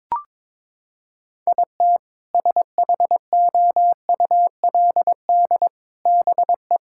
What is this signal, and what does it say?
Morse code, heard as a keyed tone, about 22 wpm, 700 Hz. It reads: IT SHOULD BE
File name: Sentences with 7-Word Phrases - 22wpm